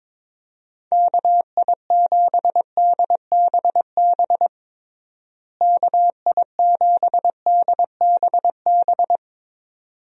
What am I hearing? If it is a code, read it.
Morse code, heard as a keyed tone, about 22 wpm, 700 Hz. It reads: KI7DBB KI7DBB
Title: USA Call Signs - 6 Characters - Repeated 2x - 22wpm